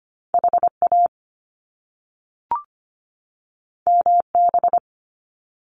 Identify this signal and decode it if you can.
Morse code, heard as a keyed tone, about 25 wpm, 700 Hz. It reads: HA MB
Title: Top Two Letter Combinations in Words 25wpm